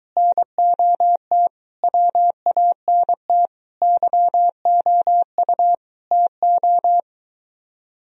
Morse code, heard as a keyed tone, about 23 wpm, 700 Hz. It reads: NOT WANT YOU TO